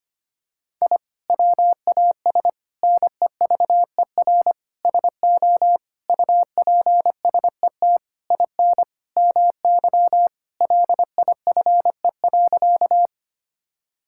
Morse code, heard as a keyed tone, about 25 wpm, 700 Hz. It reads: I WAS NEVER SO UPSET IN MY LIFE.
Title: Sentences from Top 1000 Words - Review Part 5 - 25wpm